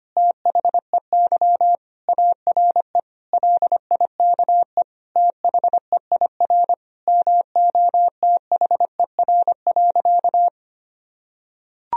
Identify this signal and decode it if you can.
Morse code, heard as a keyed tone, about 25 wpm, 700 Hz. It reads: THEY ARE LIKE THEIR MOTHER.
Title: Sentences from Top 1000 Words - Review Part 3 - 25wpm